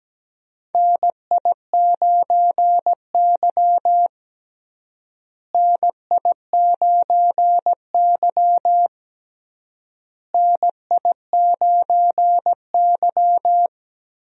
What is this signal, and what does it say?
Morse code, heard as a keyed tone, about 17 wpm, 700 Hz. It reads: NI9Y NI9Y NI9Y
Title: USA Call Signs - 4 Characters - Repeated 3x - 17wpm